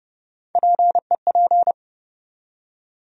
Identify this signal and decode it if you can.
Morse code, heard as a keyed tone, about 30 wpm, 700 Hz. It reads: PEP
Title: USA Call Sign Postfixes - 3 Letters 30wpm